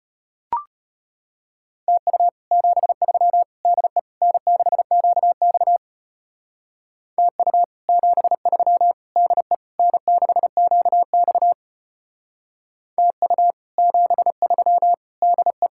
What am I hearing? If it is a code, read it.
Morse code, heard as a keyed tone, about 31 wpm, 700 Hz. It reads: TU 73 DE N6QX TU 73 DE N6QX TU 73 DE
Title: Speed-Racing Common QSO Phrases Encore - 25wpm